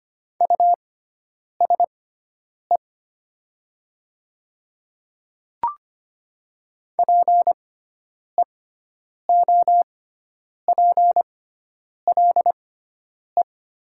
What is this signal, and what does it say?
Morse code, heard as a keyed tone, about 25 wpm, spaced as for 9 wpm, 700 Hz. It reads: USE PEOPLE
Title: 6x Character Spacing - Top 300 Words - 25wpm